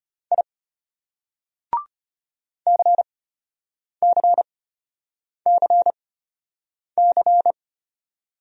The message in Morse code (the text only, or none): I CCCC